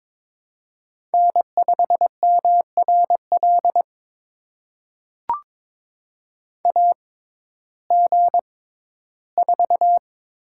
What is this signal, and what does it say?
Morse code, heard as a keyed tone, about 22 wpm, spaced as for 14 wpm, 700 Hz. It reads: N5MRL AG4
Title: USA Call Signs - 5 Characters - 6x Character Spacing - 22wpm